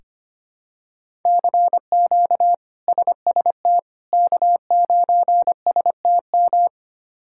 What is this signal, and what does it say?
Morse code, heard as a keyed tone, about 25 wpm, 700 Hz. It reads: CQ SST K9STM
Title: K1USN SST QSO Practice - 25wpm